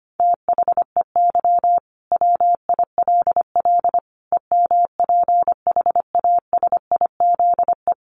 Morse code, heard as a keyed tone, about 25 wpm, 700 Hz. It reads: THEY WILL EMPHASIZE
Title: Sentences from Top 1500 Words 25wpm